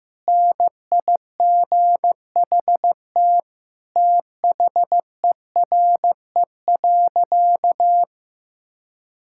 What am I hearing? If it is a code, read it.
Morse code, heard as a keyed tone, about 15 wpm, 700 Hz. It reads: NIGHT THERE.
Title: Sentences from Top 1000 Words - Review Part 5 - 15wpm